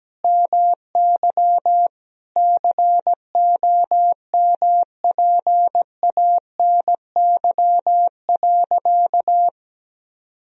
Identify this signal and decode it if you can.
Morse code, heard as a keyed tone, about 17 wpm, 700 Hz. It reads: MY COMPANY.
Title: Sentences from Top 1000 Words - Review Part 3 - 17wpm